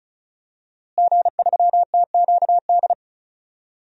Morse code, heard as a keyed tone, about 36 wpm, 700 Hz. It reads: G3TQD
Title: DXCC-100 Call Signs - 5 Characters - Repeated 3x - 35wpm